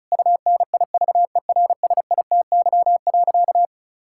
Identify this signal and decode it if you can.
Morse code, heard as a keyed tone, about 34 wpm, 700 Hz. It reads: UNIVERSITY.
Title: Sentences from Top 1000 Words - Review Part 6 - 35wpm